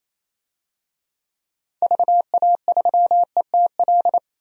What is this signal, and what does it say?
Morse code, heard as a keyed tone, about 28 wpm, 700 Hz. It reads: VA3ETL
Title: DXCC-100 Call Signs - 6 Characters - 28wpm